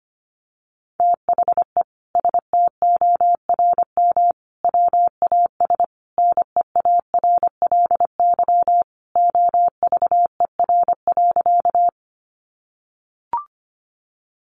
Morse code, heard as a keyed tone, about 25 wpm, 700 Hz. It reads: THE STORM WAS NEARLY OVER.
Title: Sentences from Top 1500 Words 25wpm